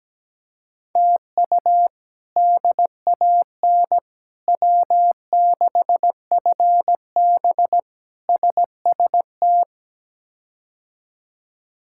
Morse code, heard as a keyed tone, about 17 wpm, 700 Hz. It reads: TU DAN W6FB SST
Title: K1USN SST QSO Practice - 17wpm